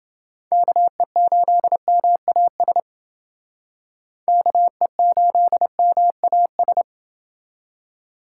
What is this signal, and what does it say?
Morse code, heard as a keyed tone, about 29 wpm, 700 Hz. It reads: KE8MAS KE8MAS